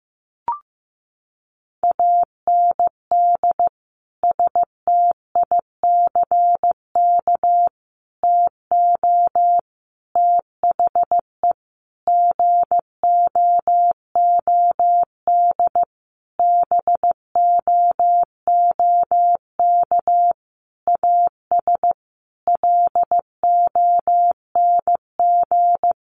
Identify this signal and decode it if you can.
Morse code, heard as a keyed tone, about 15 wpm, 700 Hz. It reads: AND STICK TO THE GOOD BOOK AS LONG
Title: Sentences from Top 1200 Words 15wpm